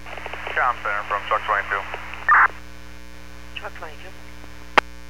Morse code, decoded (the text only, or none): none